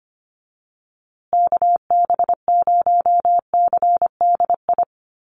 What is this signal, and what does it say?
Morse code, heard as a keyed tone, about 25 wpm, 700 Hz. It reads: KB0CDI
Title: USA Call Signs Repeated 2x 25wpm